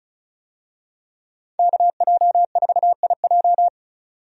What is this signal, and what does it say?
Morse code, heard as a keyed tone, about 36 wpm, 700 Hz. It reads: KJ4IJ
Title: USA Call Signs - 5 Characters - 35wpm